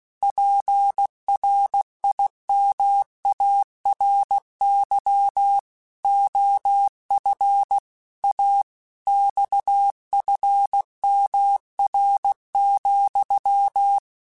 Morse code, heard as a keyed tone, about 16 wpm, 800 Hz. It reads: PRIMARY OF A XFMR,
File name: Technical Subjects / Technical Subjects-mp3s / Sw Pwr Sup @15